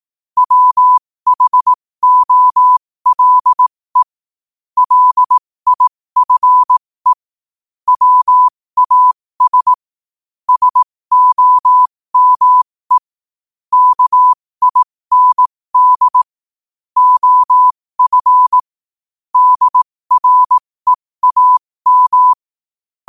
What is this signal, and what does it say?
Morse code, heard as a keyed tone, about 18 wpm, spaced as for 15 wpm, 1000 Hz. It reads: WHOLE LIFE WAS SOME KIND OF DREAM
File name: qotdfiles / 15wpm